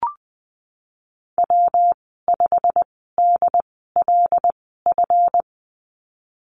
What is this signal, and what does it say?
Morse code, heard as a keyed tone, about 20 wpm, spaced as for 15 wpm, 700 Hz. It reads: W5DLF
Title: 147 Lesson 24g - Letter y - Call Signs 20wpm